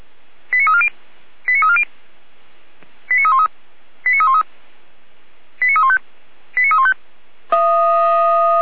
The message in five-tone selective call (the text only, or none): 89340 89340 89313 89313 89315 89315